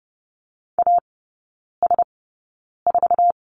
Morse code, heard as a keyed tone, about 30 wpm, spaced as for 9 wpm, 700 Hz. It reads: AS4
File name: Lesson 18e - Letter m - Three Characters 30wpm